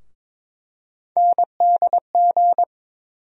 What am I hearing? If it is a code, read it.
Morse code, heard as a keyed tone, about 21 wpm, 700 Hz. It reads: NDG
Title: USA Call Sign Postfixes - 3 Letters 22wpm